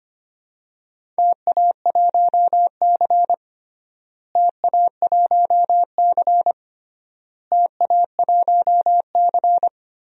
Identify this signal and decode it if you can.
Morse code, heard as a keyed tone, about 25 wpm, 700 Hz. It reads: TA1C TA1C TA1C
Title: DXCC-100 Call Signs - 4 Characters - Repeated 3x - 25wpm